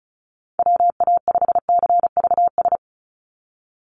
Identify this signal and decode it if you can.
Morse code, heard as a keyed tone, about 34 wpm, 700 Hz. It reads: WA5CVS
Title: USA Call Signs 35wpm